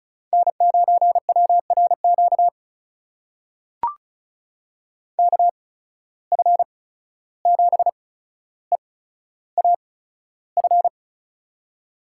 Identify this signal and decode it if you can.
Morse code, heard as a keyed tone, about 36 wpm, spaced as for 14 wpm, 700 Hz. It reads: N9WRQ KF7EAF
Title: USA Call Signs - 8x Character Spacing - 35wpm